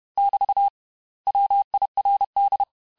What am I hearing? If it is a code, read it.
Morse code, heard as a keyed tone, about 31 wpm, 800 Hz. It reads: <BT> WIRD
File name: cours-du-mercredi / audios / cours-861-texte-les-mots-vitesse-29MPM